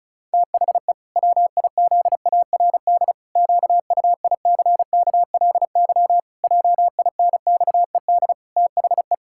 Morse code, heard as a keyed tone, about 36 wpm, 700 Hz. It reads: THE WIZARD QUICKLY JINXED THE
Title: Pangram Challenge v2 - 35wpm